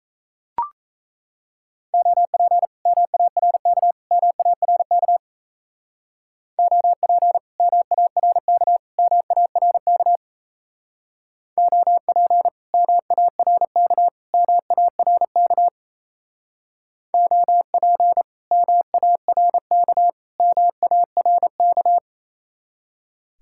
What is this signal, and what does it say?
Morse code, heard as a keyed tone, about 37 wpm, 700 Hz. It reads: OP MARK MARK OP MARK MARK OP MARK MARK OP MARK MARK